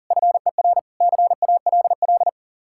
Morse code, heard as a keyed tone, about 52 wpm, 700 Hz. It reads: FER CALL